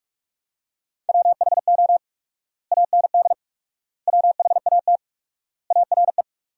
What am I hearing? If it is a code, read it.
Morse code, heard as a keyed tone, about 47 wpm, 700 Hz. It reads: WHO AND WHAT ARE
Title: 4x Word-Spacing - Sentences from Top 100 Words - 45wpm